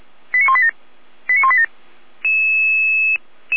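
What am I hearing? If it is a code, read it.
Five-tone selective call (ZVEI), heard as quick runs of five tones, tones of about 70 ms. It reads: 89187 89187
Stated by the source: ZVEI five-tone selective call received on SDR.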